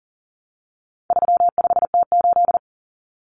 Morse code, heard as a keyed tone, about 40 wpm, 700 Hz. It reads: 35T8
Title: Common Cut Numbers - 4-Digit 40wpm